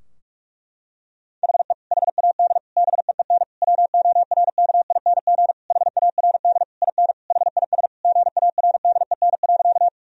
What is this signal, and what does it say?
Morse code, heard as a keyed tone, about 41 wpm, 700 Hz. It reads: HE HAD BEEN WORKING HARD IN HIS GARDEN.
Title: Sentences from Top 1000 Words - Review Part 4 - 45wpm